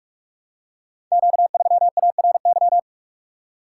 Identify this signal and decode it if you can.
Morse code, heard as a keyed tone, about 44 wpm, 700 Hz. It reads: Q3ARY
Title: Code Groups - 5 Letter:Numbers - 45wpm